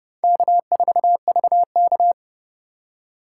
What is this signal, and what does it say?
Morse code, heard as a keyed tone, about 31 wpm, 690 Hz. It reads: K4VK
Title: USA Call Signs - 4x Character Spacing - 30wpm